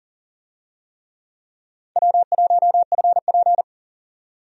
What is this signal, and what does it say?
Morse code, heard as a keyed tone, about 40 wpm, 700 Hz. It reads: W1FP
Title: USA Call Signs - 4 Characters - 40wpm